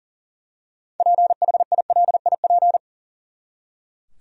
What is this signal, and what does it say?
Morse code, heard as a keyed tone, about 37 wpm, 700 Hz. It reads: PHILIP
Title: USA Common Names - 40wpm